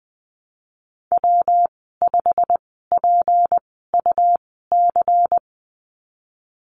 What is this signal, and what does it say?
Morse code, heard as a keyed tone, about 20 wpm, spaced as for 15 wpm, 700 Hz. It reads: W5PUC
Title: 186 Lesson 30g - Letter b - Call Signs 20wpm